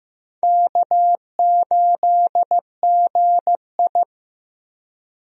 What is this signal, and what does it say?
Morse code, heard as a keyed tone, about 15 wpm, 700 Hz. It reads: K8GI
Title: USA Call Signs - 4 Characters - 15wpm